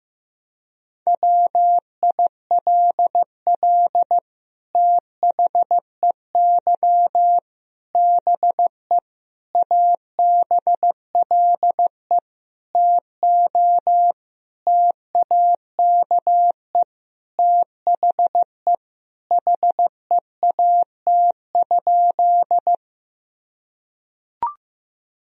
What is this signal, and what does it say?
Morse code, heard as a keyed tone, about 15 wpm, 700 Hz. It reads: WILL THEY BE ABLE TO TAKE THE HEAT?
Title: Sentences from Top 1000 Words - Review Part 3 - 15wpm